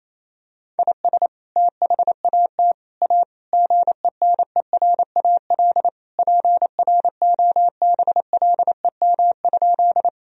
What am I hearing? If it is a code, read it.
Morse code, heard as a keyed tone, about 28 wpm, 700 Hz. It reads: IS THAT A GENERAL PROBLEM?
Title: Sentences from Top 1000 Words - Review Part 3 - 28wpm